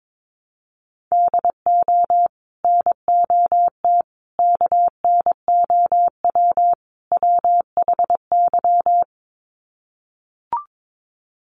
Morse code, heard as a keyed tone, about 22 wpm, 700 Hz. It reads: DO NOT KNOW WHY
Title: Sets of 4 Words 22wpm